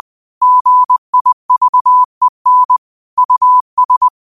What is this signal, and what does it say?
Morse code, heard as a keyed tone, about 20 wpm, 1000 Hz. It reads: GIVEN US
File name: qotdfiles / 20wpm